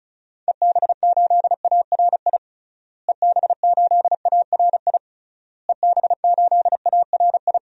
Morse code, heard as a keyed tone, about 34 wpm, 700 Hz. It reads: EB8ARI EB8ARI EB8ARI
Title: DXCC-100 Call Signs - 6 Characters - Repeated 3x - 35wpm